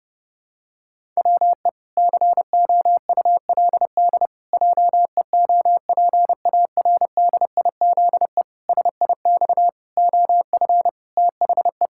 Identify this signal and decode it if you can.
Morse code, heard as a keyed tone, about 31 wpm, 700 Hz. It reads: WE COULD JEOPARDIZE SIX OF THE
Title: Pangram Challenge v2 - 30wpm